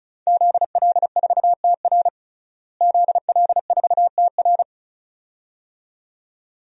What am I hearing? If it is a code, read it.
Morse code, heard as a keyed tone, about 34 wpm, 700 Hz. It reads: ZL4TR ZL4TR
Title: DXCC-100 Call Signs - 5 Characters - Repeated 2x - 35wpm